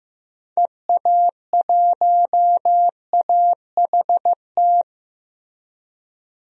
Morse code, heard as a keyed tone, about 15 wpm, 700 Hz. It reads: EA1AHT
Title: DXCC-100 Call Signs - 6 Characters - Repeated 2x - 15wpm